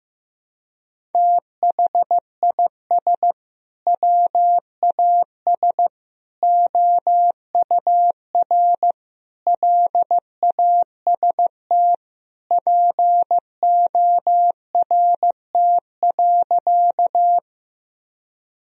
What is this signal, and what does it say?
Morse code, heard as a keyed tone, about 15 wpm, 700 Hz. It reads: THIS WAS OUR LAST PORT.
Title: Sentences from Top 1000 Words - Review Part 1 - 15wpm